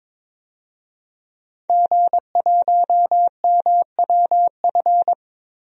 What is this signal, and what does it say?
Morse code, heard as a keyed tone, about 22 wpm, 700 Hz. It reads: G1MWF
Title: DXCC-100 Call Signs - 5 Characters - Repeated 3x - 22wpm